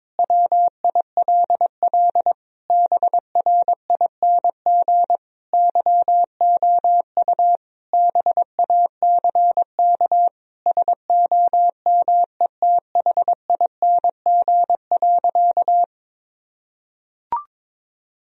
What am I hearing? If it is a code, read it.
Morse code, heard as a keyed tone, about 21 wpm, 700 Hz. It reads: WILL BRING YOU BACK SOMETHING.
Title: Sentences from Top 1000 Words - Review Part 3 - 22wpm